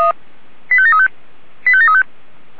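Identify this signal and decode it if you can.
Five-tone selective call (ZVEI), heard as quick runs of five tones, tones of about 70 ms. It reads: 85735 85735
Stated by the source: ZVEI five-tone selective call received by SDR.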